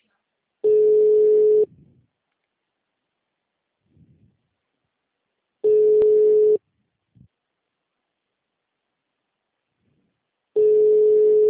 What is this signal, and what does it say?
Ringback tone (Europe, 425 Hz), with speech.
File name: out- -2044- - - .31261